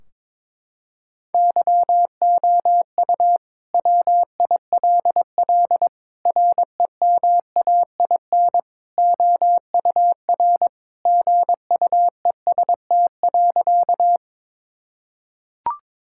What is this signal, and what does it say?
Morse code, heard as a keyed tone, about 22 wpm, 700 Hz. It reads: YOU WILL REMAIN OUR GUEST.
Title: Sentences from Top 1600 Words 22wpm